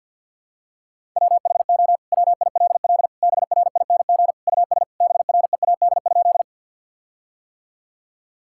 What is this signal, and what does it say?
Morse code, heard as a keyed tone, about 50 wpm, 700 Hz. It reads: WHO WILL BRING US BREAD?
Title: Sentences from Top 1600 Words 50wpm